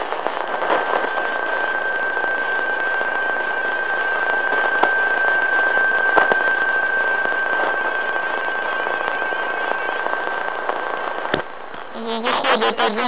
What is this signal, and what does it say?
PSK31, about 31.25 baud.